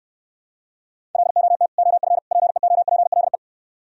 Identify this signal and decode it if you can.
Morse code, heard as a keyed tone, about 67 wpm, 700 Hz. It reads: LOT OF PEOPLE